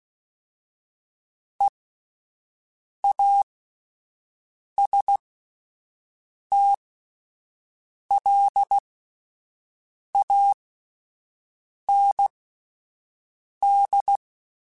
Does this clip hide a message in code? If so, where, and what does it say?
Yes: Morse code from 1.6 to 14.2 s, heard as a keyed tone, about 16 wpm, spaced as for 5 wpm, 800 Hz. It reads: EASTLAND